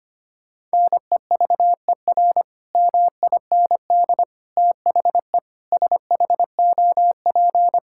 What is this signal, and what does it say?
Morse code, heard as a keyed tone, about 25 wpm, 700 Hz. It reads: NEVER MIND THE SHOP
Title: Sentences from Top 1600 Words 25wpm